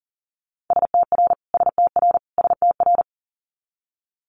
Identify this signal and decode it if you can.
Morse code, heard as a keyed tone, about 41 wpm, 700 Hz. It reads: STR STR STR
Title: Multi-letter Phonemes Repeated 3x 40wpm